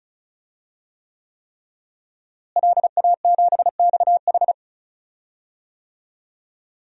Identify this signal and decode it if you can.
Morse code, heard as a keyed tone, about 36 wpm, 700 Hz. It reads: LA7XH
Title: DXCC-100 Call Signs - 5 Characters - Repeated 3x - 35wpm